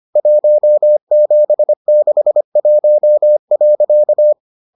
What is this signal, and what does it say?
Morse code, heard as a keyed tone, about 25 wpm, 600 Hz. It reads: 1761.